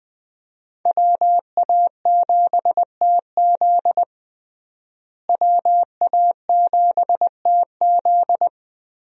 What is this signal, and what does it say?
Morse code, heard as a keyed tone, about 20 wpm, 700 Hz. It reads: WA7TZ WA7TZ
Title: USA Call Signs - 5 Characters - Repeated 2x - 20wpm